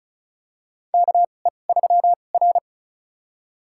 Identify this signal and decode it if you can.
Morse code, heard as a keyed tone, about 36 wpm, spaced as for 25 wpm, 700 Hz. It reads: KE3R
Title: USA Call Signs - 4 Characters - 2x Character Spacing - 35wpm